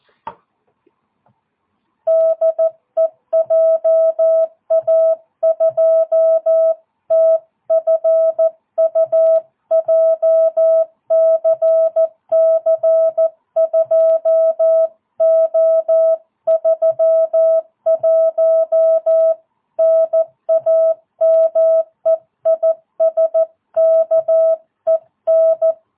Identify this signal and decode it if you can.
Morse code, heard as a keyed tone, about 14 wpm, 660 Hz. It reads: DE JA2TFU JCC 2O31 NAME IS KEN
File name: jh8cbh / 2022cw